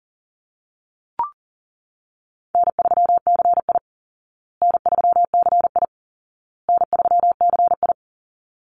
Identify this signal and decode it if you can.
Morse code, heard as a keyed tone, about 40 wpm, 700 Hz. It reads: N3CI N3CI N3CI
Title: Easy Call Signs Repeated 3x - 40wpm